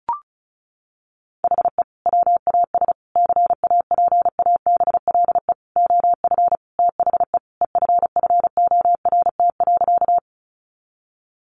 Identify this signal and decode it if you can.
Morse code, heard as a keyed tone, about 34 wpm, 700 Hz. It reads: HE WAS CAPABLE OF THE EFFORT.
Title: Sentences from Top 1200 Words 35wpm